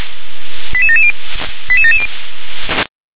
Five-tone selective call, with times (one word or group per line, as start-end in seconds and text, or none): 0.7-1.1 88770
1.7-2.1 88770